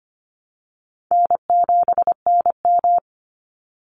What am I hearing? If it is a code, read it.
Morse code, heard as a keyed tone, about 25 wpm, 700 Hz. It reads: N7NM
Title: Call Signs Repeated 2x - 25wpm